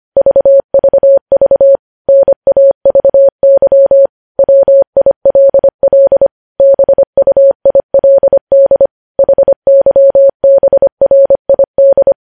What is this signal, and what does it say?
Morse code, heard as a keyed tone, about 25 wpm, 550 Hz. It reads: VVV NAVY WILL BUILD HYBRID
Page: news-headlines-in-morse-code-at-25-wpm